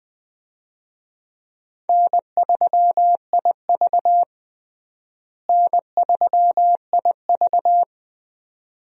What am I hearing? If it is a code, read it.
Morse code, heard as a keyed tone, about 20 wpm, 700 Hz. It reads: N3IV N3IV
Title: USA Call Signs - 4 Characters - Repeated 2x - 20wpm